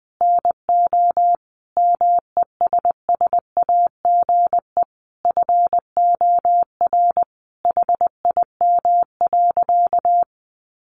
Morse code, heard as a keyed tone, about 20 wpm, 700 Hz. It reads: NO MESSAGE FOR HIM.
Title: Sentences from Top 1100 Words 20wpm